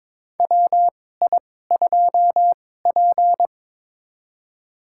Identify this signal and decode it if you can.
Morse code, heard as a keyed tone, about 22 wpm, spaced as for 16 wpm, 700 Hz. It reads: WI2P
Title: USA Call Signs - 4 Characters - 2x Character Spacing - 22wpm